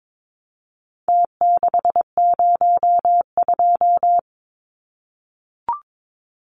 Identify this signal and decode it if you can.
Morse code, heard as a keyed tone, about 22 wpm, 700 Hz. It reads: T602
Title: Numbers - Contest Style - 4-Digit 22wpm